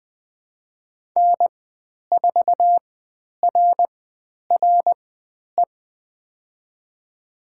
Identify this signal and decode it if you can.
Morse code, heard as a keyed tone, about 20 wpm, spaced as for 10 wpm, 700 Hz. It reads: N4RRE
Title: 095 Lesson 16g - Letter c - Call Signs 20wpm